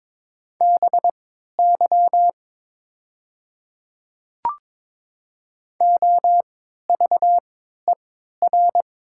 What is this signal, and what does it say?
Morse code, heard as a keyed tone, about 22 wpm, spaced as for 13 wpm, 700 Hz. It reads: BY OVER
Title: 3x Character Spacing - Top 200 Words - 22wpm